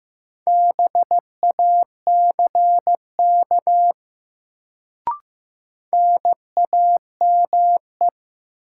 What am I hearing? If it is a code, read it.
Morse code, heard as a keyed tone, about 15 wpm, 700 Hz. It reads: BACK NAME
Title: Sentences from Top 200 Words - New Word Drill - 15wpm